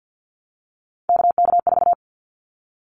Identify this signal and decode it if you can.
Morse code, heard as a keyed tone, about 50 wpm, 700 Hz. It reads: KK4
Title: Callsign Prefixes for Top 25 DXCC - Repeated 3x - 50wpm